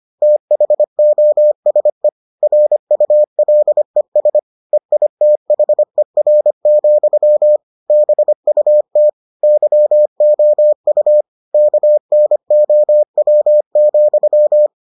Morse code, heard as a keyed tone, about 25 wpm, 600 Hz. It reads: THOSE RULES EITHER, BUT YOU KNOW,